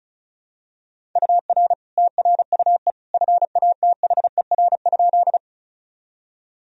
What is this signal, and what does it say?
Morse code, heard as a keyed tone, about 34 wpm, 700 Hz. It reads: UR TRUE FATHER?
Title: Sentences from Top 400 Words and Abbreviations 35wpm